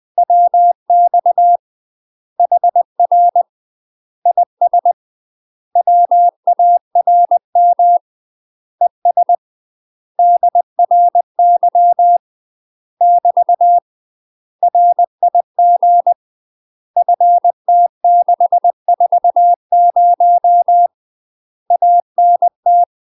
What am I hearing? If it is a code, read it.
Morse code, heard as a keyed tone, about 20 wpm, 700 Hz. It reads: WX HR IS WARM ES DRY <BT> RIG FT640 ANT
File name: UK MASTER QSOs TRAINING / 20WPM / UK MASTER QSOs 0059